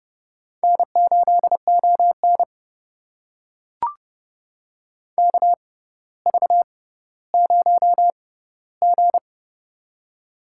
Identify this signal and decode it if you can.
Morse code, heard as a keyed tone, about 30 wpm, spaced as for 15 wpm, 700 Hz. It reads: N8ON KV0G
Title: USA Call Signs - 4 Characters - 6x Character Spacing - 30wpm